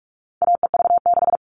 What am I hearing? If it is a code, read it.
Morse code, heard as a keyed tone, about 44 wpm, 700 Hz. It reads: AEV6
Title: All Cut Numbers - 4-Digit 45wpm